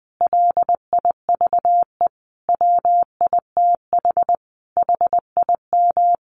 Morse code, heard as a keyed tone, about 20 wpm, 700 Hz. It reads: LIVE WITH HIM
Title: Sentences from Top 500 Words 20wpm